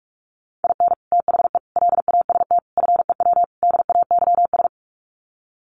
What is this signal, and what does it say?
Morse code, heard as a keyed tone, about 44 wpm, 690 Hz. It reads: IN THE LAST FEW DAYS
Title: Sets of 5 Words Encore2 45wpm